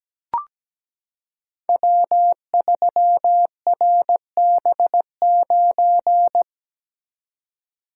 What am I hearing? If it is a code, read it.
Morse code, heard as a keyed tone, about 17 wpm, 700 Hz. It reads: W3RB9
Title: Code Groups - 5 Letter:Numbers - 17wpm